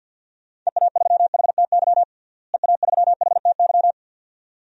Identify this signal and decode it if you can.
Morse code, heard as a keyed tone, about 47 wpm, 700 Hz. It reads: EA3HTY EA3HTY
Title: DXCC-100 Call Signs - 6 Characters - Repeated 2x - 50wpm